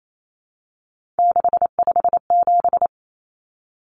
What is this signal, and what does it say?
Morse code, heard as a keyed tone, about 28 wpm, 700 Hz. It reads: 657
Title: Numbers - 3-Digit 28wpm